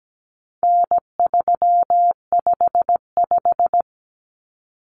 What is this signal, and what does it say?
Morse code, heard as a keyed tone, about 17 wpm, 700 Hz. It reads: N355
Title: Common Cut Numbers - 4-Digit 17wpm